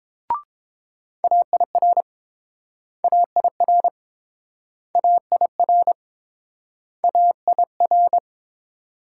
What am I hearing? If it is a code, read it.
Morse code, heard as a keyed tone, about 30 wpm, 700 Hz. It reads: AIR AIR AIR AIR